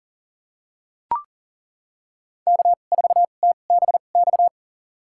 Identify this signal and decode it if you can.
Morse code, heard as a keyed tone, about 40 wpm, spaced as for 29 wpm, 700 Hz. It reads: K4TBX
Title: USA Call Signs - 5 Characters - 2x Character Spacing - 40wpm